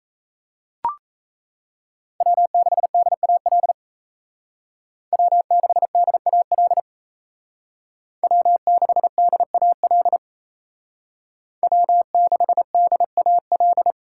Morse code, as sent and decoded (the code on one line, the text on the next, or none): .-- -.... -.. .- .-.. / .-- -.... -.. .- .-.. / .-- -.... -.. .- .-.. / .-- -.... -.. .- .-..
W6DAL W6DAL W6DAL W6DAL